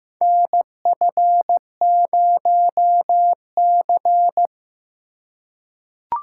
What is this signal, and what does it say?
Morse code, heard as a keyed tone, about 15 wpm, 700 Hz. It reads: NF0C
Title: USA Call Signs - 4 Characters - 3x Character Spacing - 15wpm